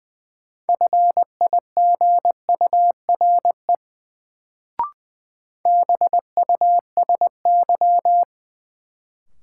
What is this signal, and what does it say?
Morse code, heard as a keyed tone, about 20 wpm, 700 Hz. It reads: FIGURE BUSY
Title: Sentences from Top 1000 Words - Review Part 2 - New Word Drill - 20wpm